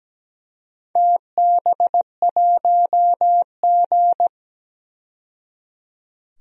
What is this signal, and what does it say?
Morse code, heard as a keyed tone, about 17 wpm, 700 Hz. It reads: TB1G
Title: DXCC-100 Call Signs - 4 Characters - 17wpm